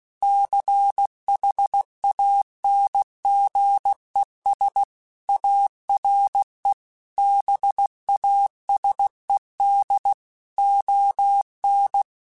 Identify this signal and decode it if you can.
Morse code, heard as a keyed tone, about 16 wpm, 800 Hz. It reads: CHANGES ARE BASED ON
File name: Technical Subjects / Technical Subjects-mp3s / Reactance @15